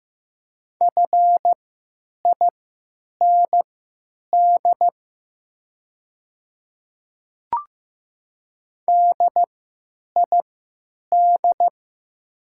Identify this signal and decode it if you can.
Morse code, heard as a keyed tone, about 15 wpm, spaced as for 8 wpm, 700 Hz. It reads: FIND DID
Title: 3x Character Spacing - Top 100 Words - 15wpm